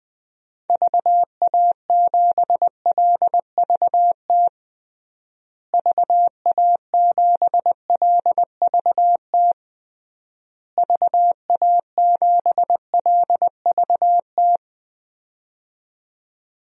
Morse code, heard as a keyed tone, about 20 wpm, 700 Hz. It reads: VA7LVT VA7LVT VA7LVT
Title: DXCC-100 Call Signs - 6 Characters - Repeated 3x - 20wpm